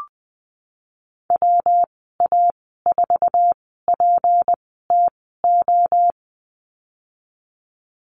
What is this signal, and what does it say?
Morse code, heard as a keyed tone, about 20 wpm, spaced as for 15 wpm, 700 Hz. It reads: WA4PTO
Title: 167 Lesson 27g - Number 7 - Call Signs 20wpm